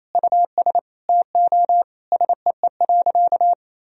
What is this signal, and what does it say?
Morse code, heard as a keyed tone, about 28 wpm, 700 Hz. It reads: US TO SEE.
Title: Sentences from Top 500 Words and Abbreviations 28wpm